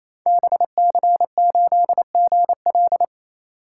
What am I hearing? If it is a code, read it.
Morse code, heard as a keyed tone, about 27 wpm, 700 Hz. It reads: BC8GL
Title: Code Groups - 5 Letter:Numbers - 28wpm